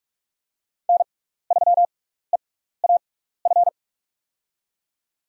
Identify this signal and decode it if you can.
Morse code, heard as a keyed tone, about 44 wpm, spaced as for 15 wpm, 700 Hz. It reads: N3EAF
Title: USA Call Signs - 6x Character Spacing - 45wpm